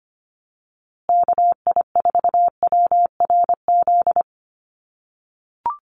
Morse code, heard as a keyed tone, about 25 wpm, 700 Hz. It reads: KI4WRZ
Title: USA Call Signs Repeated 3x 25wpm